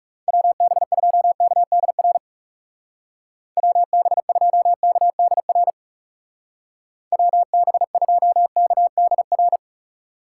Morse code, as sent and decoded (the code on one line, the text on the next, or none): .-- -... ..--- -.- -.. .-. / .-- -... ..--- -.- -.. .-. / .-- -... ..--- -.- -.. .-.
WB2KDR WB2KDR WB2KDR